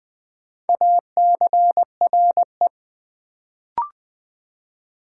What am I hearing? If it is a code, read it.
Morse code, heard as a keyed tone, about 20 wpm, 700 Hz. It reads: ACRE
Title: Build A Word - 1 Letter At A Time - 4-Letter Words - 20wpm